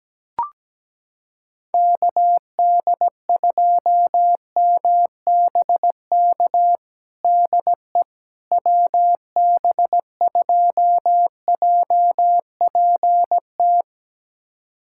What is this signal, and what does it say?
Morse code, heard as a keyed tone, about 17 wpm, 700 Hz. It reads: KD2MBK DE WB2JPT
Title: USA Callsign DE Callsign 17wpm